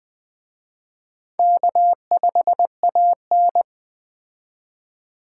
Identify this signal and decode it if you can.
Morse code, heard as a keyed tone, about 20 wpm, 700 Hz. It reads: K5AN